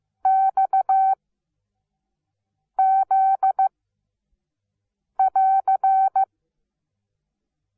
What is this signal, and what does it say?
Morse code, heard as a keyed tone, about 15 wpm, 770 Hz. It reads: X Z <AR>
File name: Onlinelehrgang / morsen / DJ4UF Morsen17